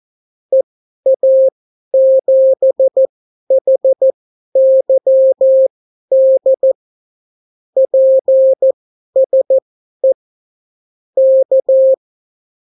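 Morse code, heard as a keyed tone, about 14 wpm, spaced as for 11 wpm, 530 Hz. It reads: EA7HYD PSE K
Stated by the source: Example Spanish CW QSO from a Morse learning book, 14 wpm characters with 8 wpm spacing.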